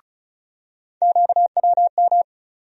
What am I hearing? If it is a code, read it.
Morse code, heard as a keyed tone, about 34 wpm, 700 Hz. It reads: QWM
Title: USA Call Sign Postfixes - 3 Letters 35wpm